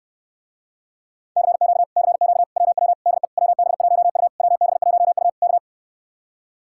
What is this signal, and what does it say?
Morse code, heard as a keyed tone, about 69 wpm, 700 Hz. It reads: CQ CQ WW DE WB1U WB1U K